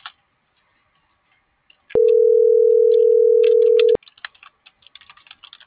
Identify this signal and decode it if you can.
Ringback tone (North America), 440 + 480 Hz.